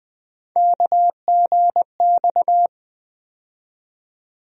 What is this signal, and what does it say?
Morse code, heard as a keyed tone, about 20 wpm, 700 Hz. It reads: KGX
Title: USA Call Sign Postfixes - 3 Letters 20wpm